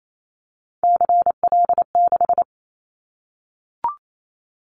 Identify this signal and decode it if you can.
Morse code, heard as a keyed tone, about 28 wpm, 700 Hz. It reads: CL6
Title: Callsign Prefixes for Top 100 DXCC - 28wpm